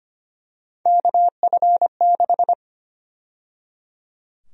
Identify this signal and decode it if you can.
Morse code, heard as a keyed tone, about 25 wpm, 700 Hz. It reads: KF6
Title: USA Call Sign Prefixes - 25wpm